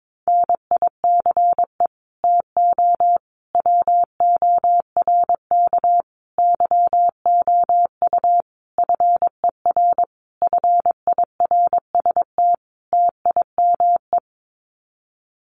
Morse code, heard as a keyed tone, about 22 wpm, 700 Hz. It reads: NICE TO WORK YOU FER FIRST TIME
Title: Common QSO Phrases 22wpm